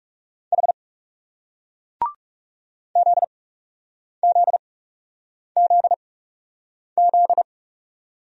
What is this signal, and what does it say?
Morse code, heard as a keyed tone, about 42 wpm, spaced as for 8 wpm, 700 Hz. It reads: H ZZZZ